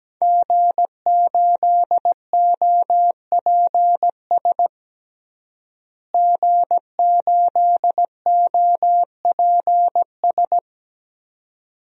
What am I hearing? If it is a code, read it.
Morse code, heard as a keyed tone, about 17 wpm, 700 Hz. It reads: G8OPS G8OPS
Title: DXCC-100 Call Signs - 5 Characters - Repeated 2x - 17wpm